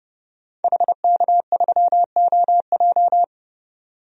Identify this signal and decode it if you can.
Morse code, heard as a keyed tone, about 30 wpm, 690 Hz. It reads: HK3OJ
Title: DXCC-100 Call Signs - 5 Characters - 30wpm